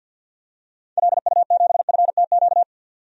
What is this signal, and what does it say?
Morse code, heard as a keyed tone, about 52 wpm, 700 Hz. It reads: LU7FTQ
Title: DXCC-100 Call Signs - 6 Characters - 50wpm